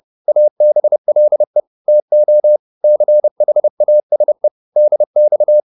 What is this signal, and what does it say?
Morse code, heard as a keyed tone, about 30 wpm, 600 Hz. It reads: ABLE TO CHASE DX